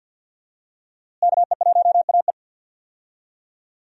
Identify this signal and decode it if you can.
Morse code, heard as a keyed tone, about 52 wpm, 700 Hz. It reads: KE1AE
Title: USA Call Signs - 5 Characters - Repeated 2x - 50wpm